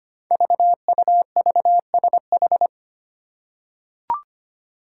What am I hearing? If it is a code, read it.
Morse code, heard as a keyed tone, about 25 wpm, 700 Hz. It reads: VUVSH
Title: Five Challenging Letter-Number Combinations 25wpm